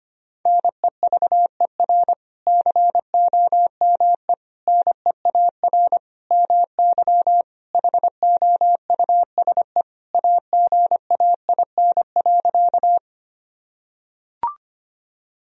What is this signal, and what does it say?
Morse code, heard as a keyed tone, about 25 wpm, 700 Hz. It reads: NEVER COME NEAR MY HOUSE AGAIN.
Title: Sentences from Top 1000 Words - Review Part 1 - 25wpm